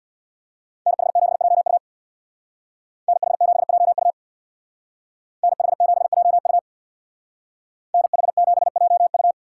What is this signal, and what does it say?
Morse code, heard as a keyed tone, about 59 wpm, 700 Hz. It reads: NH7JU NH7JU NH7JU NH7JU
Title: Speed Racing - USA Call Signs 50wpm